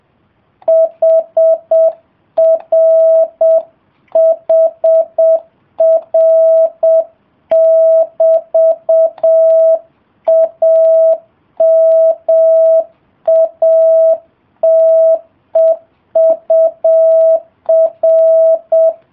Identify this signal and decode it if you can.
Morse code, heard as a keyed tone, about 7 wpm, 660 Hz. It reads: HRHR<BT>AMATEUR